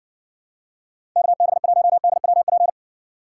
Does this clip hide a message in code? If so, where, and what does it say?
Yes: Morse code from 1.2 to 2.7 s, heard as a keyed tone, about 60 wpm, 700 Hz. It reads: KB1DWP